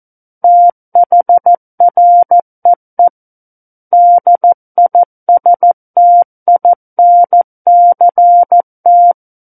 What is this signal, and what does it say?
Morse code, heard as a keyed tone, about 14 wpm, 700 Hz. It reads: THREE DISTINCT
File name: Book One / 05-Chapter 5 - 14 WPM